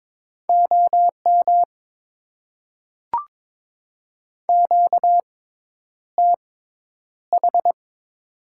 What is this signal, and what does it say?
Morse code, heard as a keyed tone, about 21 wpm, spaced as for 10 wpm, 700 Hz. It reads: OM QTH
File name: Top 90 QSO Elements - 6x Character Spacing - 22wpm